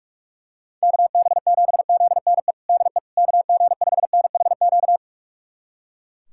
Morse code, heard as a keyed tone, about 41 wpm, 700 Hz. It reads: KB7ZNE DE KG5NHQ
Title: USA Callsign DE Callsign 45wpm